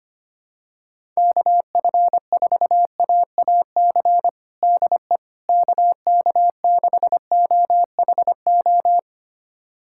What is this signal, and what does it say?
Morse code, heard as a keyed tone, about 25 wpm, 700 Hz. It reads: KF4AAC DE KK6OHO
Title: USA Callsign DE Callsign 25wpm